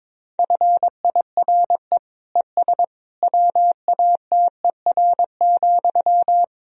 Morse code, heard as a keyed tone, about 22 wpm, 700 Hz. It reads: FIRE ES WATER,
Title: Sentences from Top 400 Words and Abbreviations 22wpm